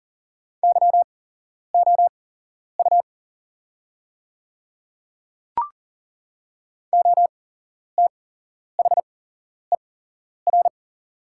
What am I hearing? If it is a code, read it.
Morse code, heard as a keyed tone, about 37 wpm, spaced as for 11 wpm, 700 Hz. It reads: YOU OTHER
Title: 8x Character Spacing - Top 100 Words - 40wpm